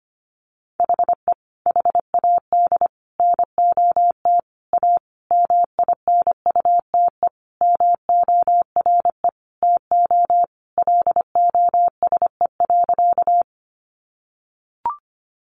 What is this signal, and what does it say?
Morse code, heard as a keyed tone, about 25 wpm, 700 Hz. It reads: HE HAD NOT A MINUTE MORE TO LOSE.
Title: Sentences from Top 1100 Words 25wpm